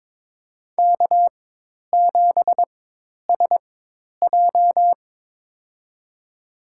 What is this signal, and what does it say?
Morse code, heard as a keyed tone, about 21 wpm, spaced as for 10 wpm, 700 Hz. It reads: K7SJ
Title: USA Call Signs - 4 Characters - 4x Character Spacing - 22wpm